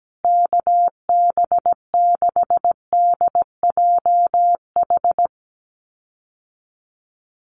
Morse code, heard as a keyed tone, about 17 wpm, 700 Hz. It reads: KB6DJH
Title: USA Call Signs Repeated 3x 17wpm